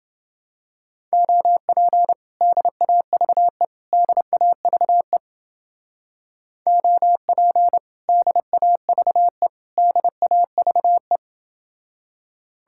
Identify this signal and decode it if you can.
Morse code, heard as a keyed tone, about 29 wpm, 700 Hz. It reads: OP DAVE DAVE OP DAVE DAVE